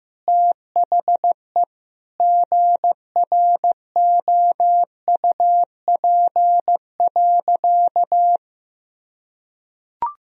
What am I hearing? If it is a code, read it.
Morse code, heard as a keyed tone, about 15 wpm, 700 Hz. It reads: THE GROUP.
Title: Sentences from Top 1000 Words - Review Part 1 - 15wpm